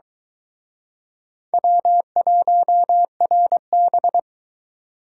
Morse code, heard as a keyed tone, about 23 wpm, 700 Hz. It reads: W1RB